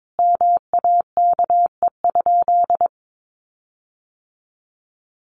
Morse code, heard as a keyed tone, about 22 wpm, 700 Hz. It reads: MAKE?
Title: Sentences from Top 1400 Words 22wpm